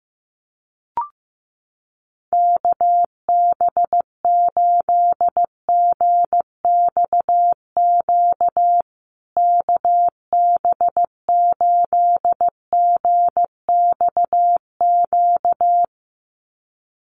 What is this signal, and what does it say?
Morse code, heard as a keyed tone, about 14 wpm, 700 Hz. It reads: KB8GXQ KB8GXQ
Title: USA Call Signs Repeated 2x 15wpm